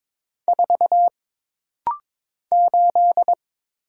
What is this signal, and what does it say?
Morse code, heard as a keyed tone, about 22 wpm, 700 Hz. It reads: 4 8
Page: This rapid-fire timing is geared to push your ICR (Instant Character Recognition) skills.